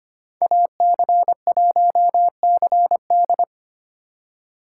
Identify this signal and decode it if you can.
Morse code, heard as a keyed tone, about 25 wpm, 700 Hz. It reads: AC1CD
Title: USA Call Signs - 5 Characters - 4x Character Spacing - 25wpm